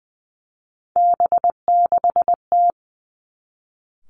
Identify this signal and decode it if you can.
Morse code, heard as a keyed tone, about 20 wpm, 700 Hz. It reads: B6T
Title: Three Challenging Letter-Number Combinations 20wpm